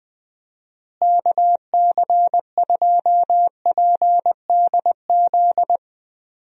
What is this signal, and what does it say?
Morse code, heard as a keyed tone, about 20 wpm, 700 Hz. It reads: KC2PDZ
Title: USA Call Signs - 6 Characters - Repeated 3x - 20wpm